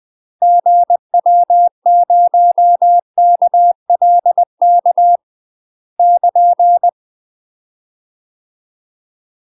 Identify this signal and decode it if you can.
Morse code, heard as a keyed tone, about 20 wpm, 700 Hz. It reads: GW0KLK <KN>
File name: UK MASTER QSOs TRAINING/20WPM